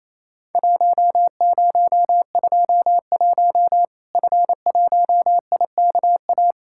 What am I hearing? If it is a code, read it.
Morse code, heard as a keyed tone, about 28 wpm, 700 Hz. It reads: 1021 F1IKA
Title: Worked All Europe DX Contest - QTC Exchange - 28wpm